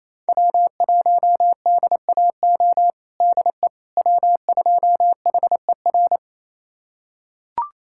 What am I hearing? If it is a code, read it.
Morse code, heard as a keyed tone, about 27 wpm, 700 Hz. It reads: W1DAO DE W2HER
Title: USA Callsign DE Callsign - 5 Characters - 28wpm